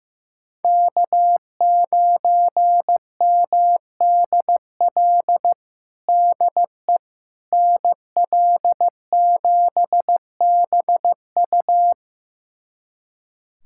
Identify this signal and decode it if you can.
Morse code, heard as a keyed tone, about 15 wpm, 700 Hz. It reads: K9MDL DE NL7BU
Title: USA Callsign DE Callsign - 5 Characters - 15wpm